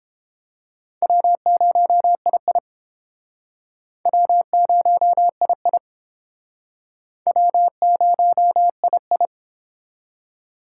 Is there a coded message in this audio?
Yes: Morse code from 1.0 to 9.3 s, heard as a keyed tone, about 31 wpm, 700 Hz. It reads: W0II W0II W0II